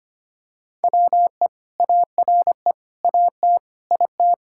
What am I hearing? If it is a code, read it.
Morse code, heard as a keyed tone, about 25 wpm, 700 Hz. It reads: WE ARE AT IT
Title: Sentences from Top 1000 Words - Review Part 5 - 25wpm